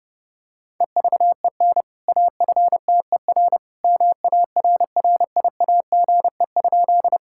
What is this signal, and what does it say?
Morse code, heard as a keyed tone, about 30 wpm, 700 Hz. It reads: EVEN AFTER MARRIAGE?
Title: Sentences from Top 1000 Words - Review Part 5 - 30wpm